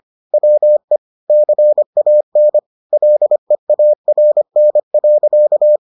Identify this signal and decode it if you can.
Morse code, heard as a keyed tone, about 25 wpm, 600 Hz. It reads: WE CAN LEARN.